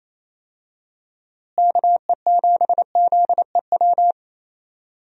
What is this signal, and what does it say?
Morse code, heard as a keyed tone, about 28 wpm, 700 Hz. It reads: KE7ZEW
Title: USA Call Signs - 6 Characters - Repeated 3x - 28wpm